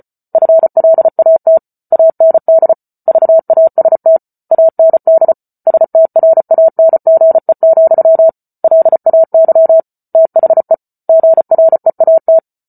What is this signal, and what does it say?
Morse code, heard as a keyed tone, about 34 wpm, 650 Hz. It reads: FLAT AND VAST AND STRANGE, LAY THE GREAT